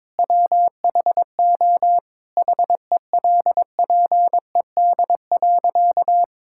Morse code, heard as a keyed tone, about 22 wpm, 700 Hz. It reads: WHO HELPED.
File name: practice / 22wpm / Sentences with 4-Word Phrases - 22wpm